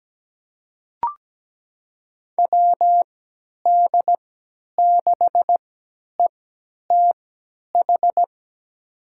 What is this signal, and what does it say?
Morse code, heard as a keyed tone, about 17 wpm, spaced as for 10 wpm, 700 Hz. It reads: WD6ETH
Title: USA Call Signs - 6 Characters - 3x Character Spacing - 17wpm